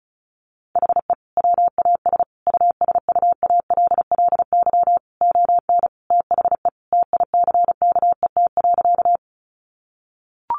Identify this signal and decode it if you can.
Morse code, heard as a keyed tone, about 34 wpm, 700 Hz. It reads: HE WAS USUALLY ON THE TICKET.
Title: Sentences from Top 1400 Words 35wpm